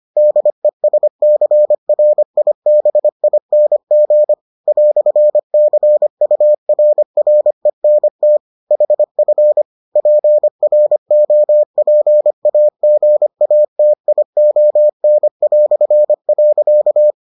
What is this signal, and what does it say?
Morse code, heard as a keyed tone, about 25 wpm, 600 Hz. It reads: DESCRIBING "CURRENT HF PROPAGATION".